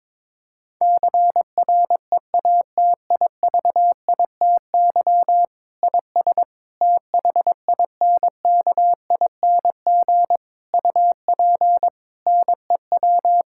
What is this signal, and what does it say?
Morse code, heard as a keyed tone, about 22 wpm, 700 Hz. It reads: CREATIVITY IS THINKING UP NEW
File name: Difficult Sentences with Hints - 22wpm